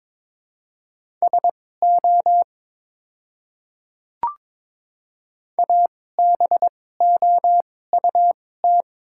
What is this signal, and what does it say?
Morse code, heard as a keyed tone, about 22 wpm, spaced as for 16 wpm, 700 Hz. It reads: SO ABOUT
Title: 2x Character Spacing - Top 100 Words - 22wpm